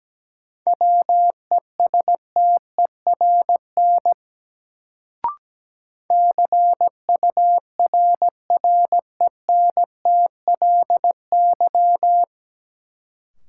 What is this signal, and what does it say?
Morse code, heard as a keyed tone, about 17 wpm, 700 Hz. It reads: WESTERN CURRENTLY
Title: Sentences from Top 1000 Words - Review Part 7 - New Word Drill - 17wpm